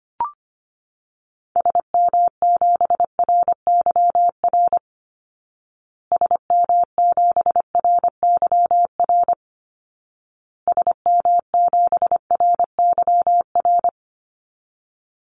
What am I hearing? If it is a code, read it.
Morse code, heard as a keyed tone, about 25 wpm, 700 Hz. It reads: SM7RYR SM7RYR SM7RYR
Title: Call Signs Repeated 2x - 25wpm